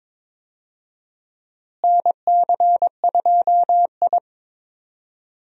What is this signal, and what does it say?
Morse code, heard as a keyed tone, about 21 wpm, 700 Hz. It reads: NC2I
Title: USA Call Signs - 4 Characters - Repeated 2x - 22wpm